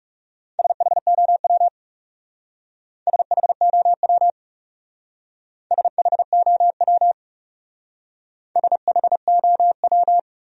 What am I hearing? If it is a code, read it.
Morse code, heard as a keyed tone, about 36 wpm, 700 Hz. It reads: SHOW SHOW SHOW SHOW
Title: Speed-Racing Top 500 Words 30wpm